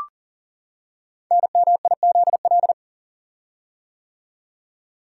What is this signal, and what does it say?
Morse code, heard as a keyed tone, about 41 wpm, 700 Hz. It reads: NMIZL
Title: Code Groups - 5 Letters - 40wpm